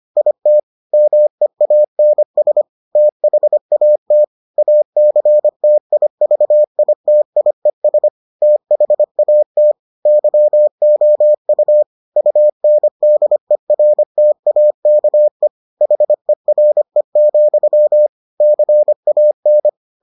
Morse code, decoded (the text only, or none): IT MEANS THAT ACTIVITIES THAT YOU UNDERTAKE HERE, CAN